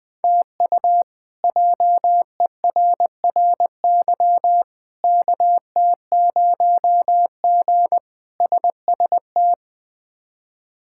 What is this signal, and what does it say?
Morse code, heard as a keyed tone, about 20 wpm, 700 Hz. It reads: TU JERRY KT0G SST
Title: K1USN SST QSO Practice - 20wpm